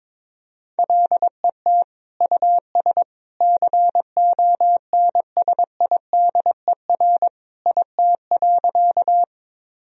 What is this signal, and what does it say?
Morse code, heard as a keyed tone, about 22 wpm, 700 Hz. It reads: LET US CONSIDER IT.
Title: Sentences from Top 1000 Words - Review Part 4 - 22wpm